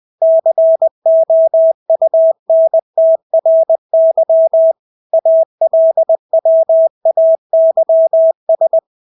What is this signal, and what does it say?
Morse code, heard as a keyed tone, about 20 wpm, 650 Hz. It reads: COUNTRY ALWAYS
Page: Engelske ord 20 wpm